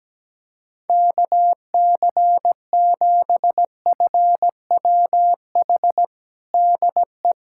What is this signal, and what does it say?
Morse code, heard as a keyed tone, about 17 wpm, 700 Hz. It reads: KC7FWH DE
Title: USA Callsign DE Callsign - 6 Characters - 17wpm